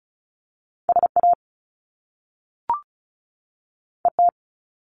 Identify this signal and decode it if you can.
Morse code, heard as a keyed tone, about 36 wpm, 700 Hz. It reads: SA ET
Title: Top Two Letter Combinations in Words 35wpm